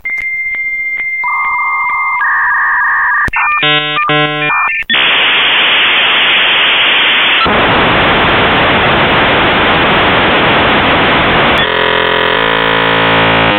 Dial-up modem handshake.